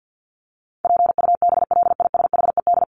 Morse code, heard as a keyed tone, about 50 wpm, 700 Hz. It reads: PUBLISHED
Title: ED Suffix - Top 100 Words with Base Words 50wpm